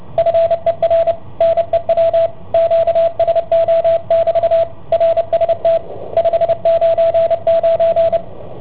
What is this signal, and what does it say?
Morse code, heard as a keyed tone, about 29 wpm, 660 Hz, with radio noise. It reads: FER NEW QSO<BT> RST 599